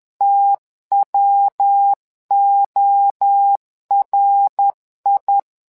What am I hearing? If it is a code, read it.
Morse code, heard as a keyed tone, about 11 wpm, 800 Hz. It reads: TWORI
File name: LEZIONI / 3 MWRO / 3 MWRO 10wpm 2148